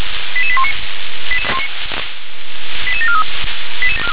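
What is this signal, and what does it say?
Five-tone selective call (ZVEI), heard as quick runs of five tones, tones of about 70 ms. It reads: 88819 88743
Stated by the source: ZVEI five-tone selective call, received by SDR.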